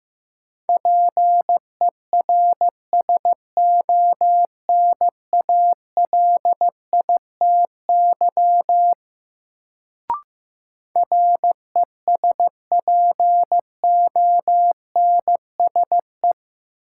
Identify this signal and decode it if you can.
Morse code, heard as a keyed tone, about 15 wpm, 700 Hz. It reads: PERSONALITY RESPONSE
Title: Sentences from Top 1400 Words - New Word Drill - 15wpm